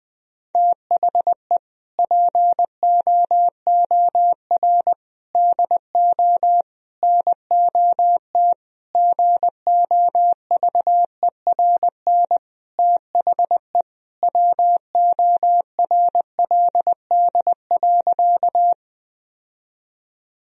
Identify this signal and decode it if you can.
Morse code, heard as a keyed tone, about 20 wpm, 700 Hz. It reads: THE POOR DO NOT GOVERN THE WORLD.
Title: Sentences from Top 1000 Words - Review Part 2 - 20wpm